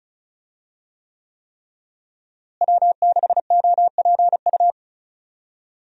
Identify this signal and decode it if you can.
Morse code, heard as a keyed tone, about 36 wpm, 700 Hz. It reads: W6OPU
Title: USA Call Signs - 5 Characters - Repeated 3x - 35wpm